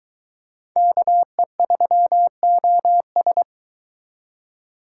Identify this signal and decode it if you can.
Morse code, heard as a keyed tone, about 24 wpm, 700 Hz. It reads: KE3OS